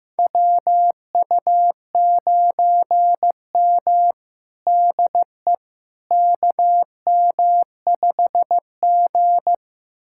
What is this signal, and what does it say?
Morse code, heard as a keyed tone, about 15 wpm, 700 Hz. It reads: WU9M DE KM5G
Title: USA Callsign DE Callsign - 4 Characters - 15wpm